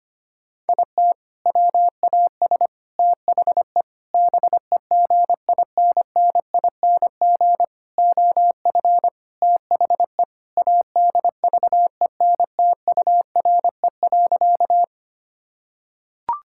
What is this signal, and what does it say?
Morse code, heard as a keyed tone, about 25 wpm, 700 Hz. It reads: IT WAS THE BEGINNING OF THE ADVENTURE.
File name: practice / 25wpm / Sentences with 4-Word Phrases - 25wpm